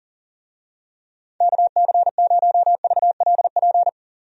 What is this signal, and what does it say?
Morse code, heard as a keyed tone, about 37 wpm, 700 Hz. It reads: KC0VLP
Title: USA Call Signs - 6 Characters - Repeated 2x - 40wpm